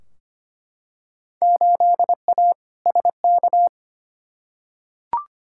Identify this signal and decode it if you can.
Morse code, heard as a keyed tone, about 25 wpm, 700 Hz. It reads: 8A SK
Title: Field Day Exchanges Encore - 25wpm